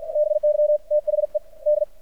Morse code, heard as a keyed tone, about 53 wpm, 600 Hz. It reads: BY THE D
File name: archived / CRSnet